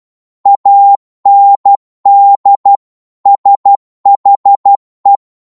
Morse code, heard as a keyed tone, about 12 wpm, 800 Hz. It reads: AND SHE
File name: Morse / psg-12wpm